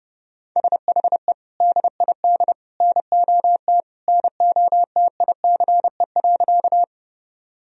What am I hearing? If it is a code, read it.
Morse code, heard as a keyed tone, about 30 wpm, 700 Hz. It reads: SHE DID NOT NOTICE.
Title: Sentences from Top 1000 Words - Review Part 2 - 30wpm